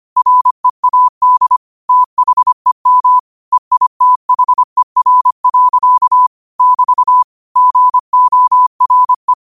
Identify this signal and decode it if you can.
Morse code, heard as a keyed tone, about 25 wpm, 1000 Hz. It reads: READ THEM EITHER. <BT> GORE
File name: qotdfiles / 25wpm